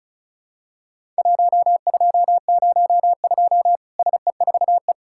Morse code, heard as a keyed tone, about 36 wpm, 700 Hz. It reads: 1202 SE4E
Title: Worked All Europe DX Contest - QTC Exchange - 35wpm